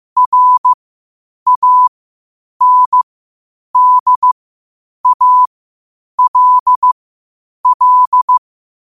Morse code, heard as a keyed tone, about 15 wpm, spaced as for 8 wpm, 1000 Hz. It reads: RANDALL
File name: qotdfiles / 8wpm